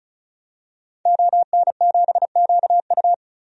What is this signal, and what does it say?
Morse code, heard as a keyed tone, about 36 wpm, 700 Hz. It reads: ON7QU
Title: DXCC-100 Call Signs - 5 Characters - 35wpm